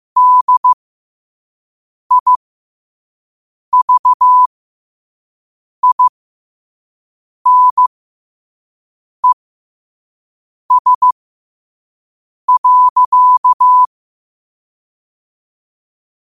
Morse code, heard as a keyed tone, about 15 wpm, spaced as for 5 wpm, 1000 Hz. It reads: DIVINES.
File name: qotdfiles / 5wpm